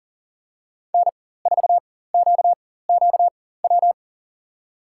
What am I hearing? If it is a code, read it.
Morse code, heard as a keyed tone, about 40 wpm, spaced as for 19 wpm, 700 Hz. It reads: N4QQW
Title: USA Call Signs - 5 Characters - 4x Character Spacing - 40wpm